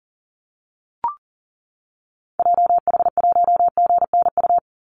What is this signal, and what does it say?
Morse code, heard as a keyed tone, about 40 wpm, 700 Hz. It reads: JH1GNU
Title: Call Signs - 40wpm